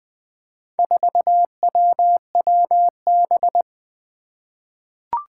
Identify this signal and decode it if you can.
Morse code, heard as a keyed tone, about 20 wpm, 700 Hz. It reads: 4WWB
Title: Four Challenging Letter-Number Combinations 20wpm